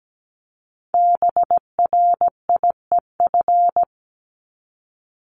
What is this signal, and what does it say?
Morse code, heard as a keyed tone, about 17 wpm, 700 Hz. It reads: BRIEF
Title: LY Suffix - Top 100 Words with Base Words 17wpm